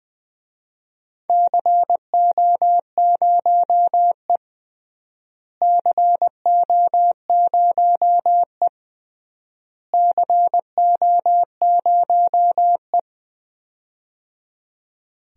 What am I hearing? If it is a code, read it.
Morse code, heard as a keyed tone, about 20 wpm, 700 Hz. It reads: CO0E CO0E CO0E
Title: DXCC-100 Call Signs - 4 Characters - Repeated 3x - 20wpm